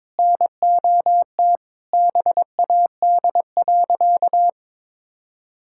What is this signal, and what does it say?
Morse code, heard as a keyed tone, about 22 wpm, 700 Hz. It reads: NOT BAD.
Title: Sentences from Top 1000 Words - Review Part 5 - 22wpm